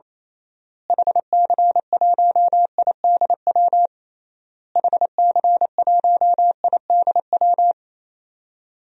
Morse code, heard as a keyed tone, about 28 wpm, 700 Hz. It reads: HC1IDW HC1IDW
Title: DXCC-100 Call Signs - 6 Characters - Repeated 3x - 28wpm